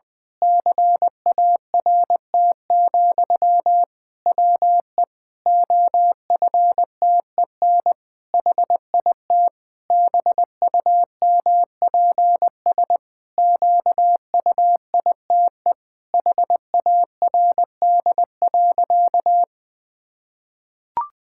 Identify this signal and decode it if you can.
Morse code, heard as a keyed tone, about 20 wpm, 700 Hz. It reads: CART, WE OFTEN HIT BUMPS QUITE HARD.
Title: Pangram Challenge v2 - 20wpm